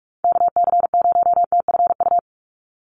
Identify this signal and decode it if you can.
Morse code, heard as a keyed tone, about 44 wpm, 700 Hz. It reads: KC0TFU
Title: USA Call Signs 45wpm